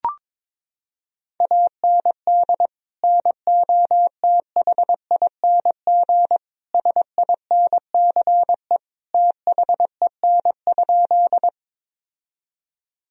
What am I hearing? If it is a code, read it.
Morse code, heard as a keyed tone, about 22 wpm, 700 Hz. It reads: AND NOTHING SINCE THEN?
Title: Sentences from Top 1000 Words - Review Part 1 - 22wpm